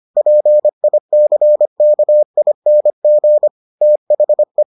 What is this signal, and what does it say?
Morse code, heard as a keyed tone, about 25 wpm, 600 Hz. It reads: PICKING THE